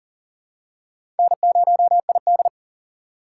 Morse code, heard as a keyed tone, about 40 wpm, 700 Hz. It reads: N0ID
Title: USA Call Signs - 4 Characters - 4x Character Spacing - 40wpm